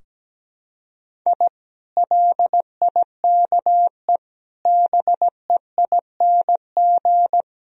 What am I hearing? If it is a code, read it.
Morse code, heard as a keyed tone, about 17 wpm, 700 Hz. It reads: I LIKE BEING
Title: Sentences from Top 1000 Words - Review Part 3 - 17wpm